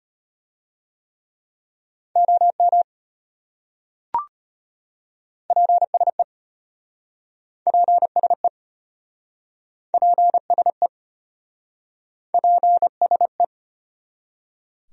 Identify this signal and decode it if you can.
Morse code, heard as a keyed tone, about 33 wpm, 700 Hz. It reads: OM PSE PSE PSE PSE